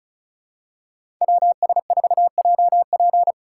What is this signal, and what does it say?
Morse code, heard as a keyed tone, about 34 wpm, 700 Hz. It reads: WS4JP
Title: USA Call Signs - 5 Characters - 3x Character Spacing - 35wpm